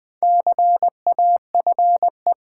Morse code, heard as a keyed tone, about 20 wpm, 700 Hz. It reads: CAFE
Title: Build A Word - 1 Letter At A Time - 4-Letter Words - 20wpm